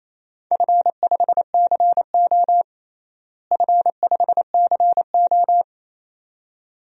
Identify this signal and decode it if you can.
Morse code, heard as a keyed tone, about 28 wpm, 700 Hz. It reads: F5CO F5CO
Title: DXCC-100 Call Signs - 4 Characters - Repeated 2x - 28wpm